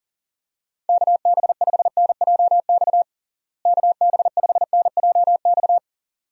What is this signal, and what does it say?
Morse code, heard as a keyed tone, about 40 wpm, 700 Hz. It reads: KB5NJX KB5NJX
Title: USA Call Signs - 6 Characters - Repeated 2x - 40wpm